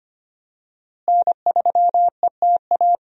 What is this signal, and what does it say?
Morse code, heard as a keyed tone, about 25 wpm, 700 Hz. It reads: N3ETA
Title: USA Call Signs - 5 Characters - Repeated 2x - 25wpm